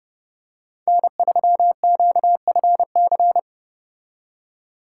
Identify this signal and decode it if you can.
Morse code, heard as a keyed tone, about 29 wpm, 700 Hz. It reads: N3QFC